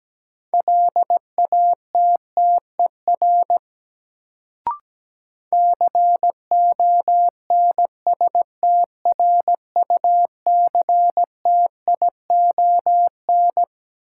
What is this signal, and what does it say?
Morse code, heard as a keyed tone, about 17 wpm, 700 Hz. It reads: LATTER CONSTRUCTION
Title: Sentences from Top 1600 Words - New Word Drill - 17wpm